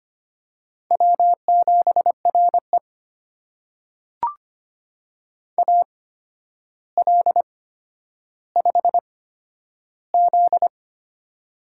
Morse code, heard as a keyed tone, about 25 wpm, spaced as for 11 wpm, 700 Hz. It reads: W7RE AL5Z
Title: USA Call Signs - 4 Characters - 8x Character Spacing - 25wpm